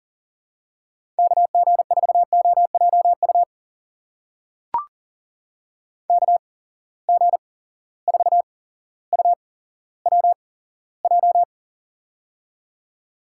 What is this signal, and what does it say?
Morse code, heard as a keyed tone, about 41 wpm, spaced as for 17 wpm, 700 Hz. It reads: KG4OJU KG4UWJ
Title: USA Call Signs - 6 Characters - 8x Character Spacing - 40wpm